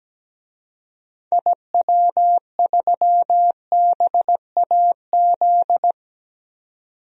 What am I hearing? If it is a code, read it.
Morse code, heard as a keyed tone, about 17 wpm, 700 Hz. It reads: IW3BAZ
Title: DXCC-100 Call Signs - 6 Characters - 17wpm